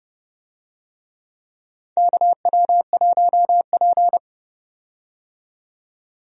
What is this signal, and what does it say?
Morse code, heard as a keyed tone, about 31 wpm, 700 Hz. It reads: KW1P
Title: USA Call Signs - 4 Characters - 30wpm